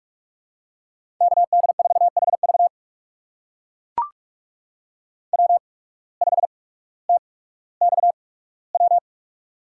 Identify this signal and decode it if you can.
Morse code, heard as a keyed tone, about 41 wpm, spaced as for 19 wpm, 700 Hz. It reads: KD4HV W5TXW
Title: USA Call Signs - 5 Characters - 8x Character Spacing - 45wpm